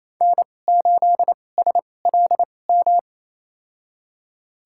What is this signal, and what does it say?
Morse code, heard as a keyed tone, about 28 wpm, spaced as for 20 wpm, 700 Hz. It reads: N8SLM
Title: USA Call Signs - 5 Characters - 2x Character Spacing - 28wpm